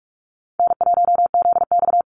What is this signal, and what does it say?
Morse code, heard as a keyed tone, about 44 wpm, 700 Hz. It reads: N1ZX
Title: Easy Call Signs Repeated 2x - 45wpm